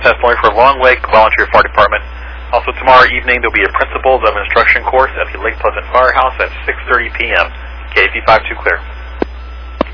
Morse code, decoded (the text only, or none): none